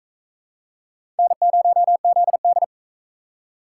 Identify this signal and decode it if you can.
Morse code, heard as a keyed tone, about 40 wpm, 700 Hz. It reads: N0ZD